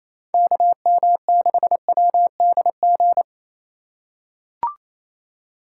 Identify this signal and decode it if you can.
Morse code, heard as a keyed tone, about 28 wpm, 700 Hz. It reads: KM6WDG
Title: USA Call Signs - 6 Characters - 2x Character Spacing - 28wpm